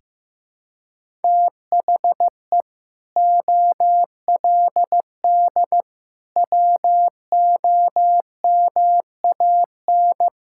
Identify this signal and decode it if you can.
Morse code, heard as a keyed tone, about 15 wpm, 700 Hz. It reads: THE OLD WOMAN
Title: Sentences from Top 1600 Words 15wpm